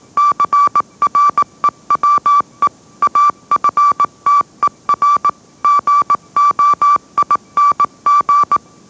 Morse code, heard as a keyed tone, about 18 wpm, 1160 Hz. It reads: CREWE AFTER GOING
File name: mrx / mrx081 20wpm